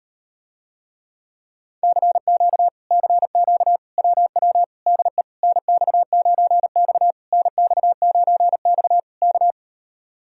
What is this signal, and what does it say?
Morse code, heard as a keyed tone, about 38 wpm, 700 Hz. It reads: CQ CQ WW DE NX9X NX9X K